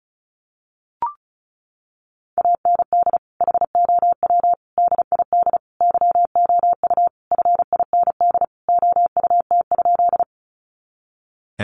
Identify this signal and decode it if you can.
Morse code, heard as a keyed tone, about 34 wpm, 700 Hz. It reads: AND HOW DID YOU FIND OUT?
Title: Sentences from Top 200 Words 35wpm